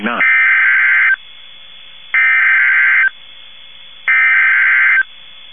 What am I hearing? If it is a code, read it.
Emergency Alert System (SAME) tones (header bursts), heard as harsh buzzing bursts, with speech. It is heard from 0.2 to 5.0 s.
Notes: emitido por sistema NOAA - Radio KEC50 por 162,475 Mhz